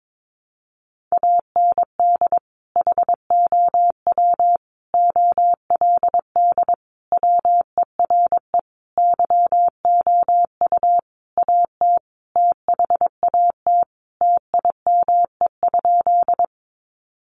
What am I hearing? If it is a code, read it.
Morse code, heard as a keyed tone, about 22 wpm, 700 Hz. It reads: AND HOW OLD WERE YOU AT THAT TIME?
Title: Sentences from Top 200 Words 22wpm